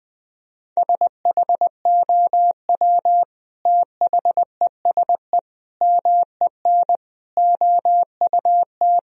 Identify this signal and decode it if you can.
Morse code, heard as a keyed tone, about 20 wpm, 700 Hz. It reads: SHOW THESE MEN OUT
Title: Sentences from Top 1000 Words - Review Part 1 - 20wpm